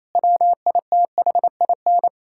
Morse code, heard as a keyed tone, about 28 wpm, 700 Hz. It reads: WITHIN
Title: Sentences from Top 1000 Words - Review Part 5 - 28wpm